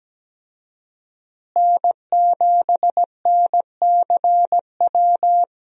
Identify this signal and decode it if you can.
Morse code, heard as a keyed tone, about 17 wpm, 700 Hz. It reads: N7NCW
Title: USA Call Signs - 5 Characters - Repeated 2x - 17wpm